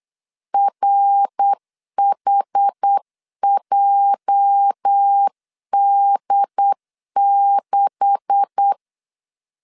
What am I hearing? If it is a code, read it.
Morse code, heard as a keyed tone, about 8 wpm, 800 Hz. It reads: RHJD6
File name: LEZIONI / 9 286 8wpm 3425